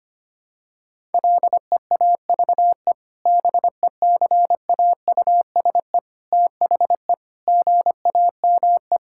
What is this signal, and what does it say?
Morse code, heard as a keyed tone, about 25 wpm, 700 Hz. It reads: LEAVE BECAUSE THE GAME
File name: practice / 25wpm / Sentences with 2-Word Phrases - 25wpm